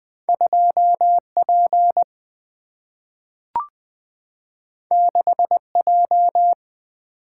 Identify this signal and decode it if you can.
Morse code, heard as a keyed tone, about 20 wpm, 700 Hz. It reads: 2P 6J
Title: Single-Letter Single-Number 20wpm